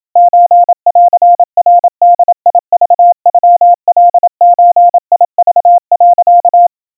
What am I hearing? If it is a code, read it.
Morse code (Japanese Wabun code), heard as a keyed tone, about 28 wpm, 700 Hz. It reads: ソンナボクノカゾク、
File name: kawa / Morse sample / kamyu 100 cpm 700 Hz Crispy